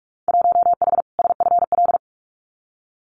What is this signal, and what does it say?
Morse code, heard as a keyed tone, about 44 wpm, 700 Hz. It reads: 1H SFL
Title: Winter Field Day Exchanges 45wpm